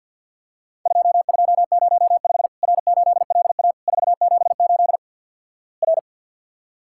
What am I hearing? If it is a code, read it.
Morse code, heard as a keyed tone, about 51 wpm, 700 Hz. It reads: 2205 R8LA 478 R
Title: Worked All Europe DX Contest - QTC Exchange - 50wpm